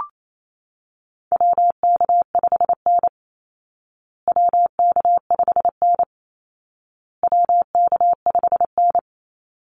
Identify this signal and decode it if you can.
Morse code, heard as a keyed tone, about 28 wpm, 700 Hz. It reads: WK5N WK5N WK5N
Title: Call Signs Repeated 2x - 28wpm